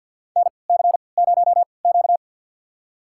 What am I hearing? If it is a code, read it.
Morse code, heard as a keyed tone, about 51 wpm, spaced as for 28 wpm, 700 Hz. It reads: NC0Q
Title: USA Call Signs - 4 Characters - 3x Character Spacing - 50wpm